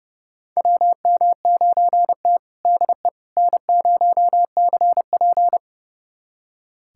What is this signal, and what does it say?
Morse code, heard as a keyed tone, about 30 wpm, 700 Hz. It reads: WM9T DE N0CP
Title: USA Callsign DE Callsign - 4 Characters - 30wpm